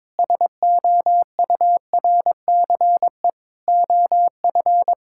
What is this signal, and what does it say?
Morse code, heard as a keyed tone, about 22 wpm, 700 Hz. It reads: SOURCE OF
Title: Sentences from Top 1000 Words - Review Part 5 - 22wpm